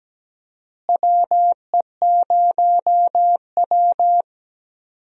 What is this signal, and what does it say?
Morse code, heard as a keyed tone, about 17 wpm, 700 Hz. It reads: WE0W
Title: USA Call Signs - 4 Characters - Repeated 3x - 17wpm